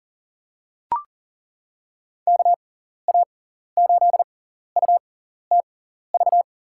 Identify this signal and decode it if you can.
Morse code, heard as a keyed tone, about 40 wpm, spaced as for 14 wpm, 700 Hz. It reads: KA8UTV
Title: USA Call Signs - 6x Character Spacing - 40wpm